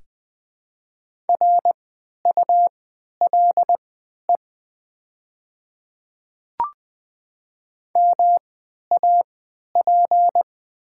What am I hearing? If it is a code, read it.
Morse code, heard as a keyed tone, about 20 wpm, spaced as for 11 wpm, 700 Hz. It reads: RULE MAP
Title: 3x Character Spacing - Top 500 Words - 20wpm